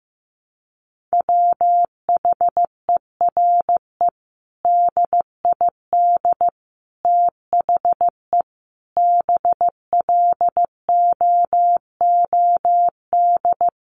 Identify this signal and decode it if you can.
Morse code, heard as a keyed tone, about 15 wpm, 700 Hz. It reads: WHERE DID THE BLOOD
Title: Sentences from Top 1100 Words 15wpm